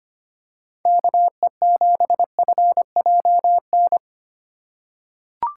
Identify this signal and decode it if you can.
Morse code, heard as a keyed tone, about 25 wpm, 700 Hz. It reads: KE7FJN
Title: USA Call Signs - 6 Characters - 3x Character Spacing - 25wpm